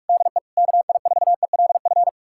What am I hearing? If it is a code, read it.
Morse code, heard as a keyed tone, about 69 wpm, spaced as for 56 wpm, 700 Hz. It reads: DE KI4ELF
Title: Speed-Racing Common QSO Phrases - 45wpm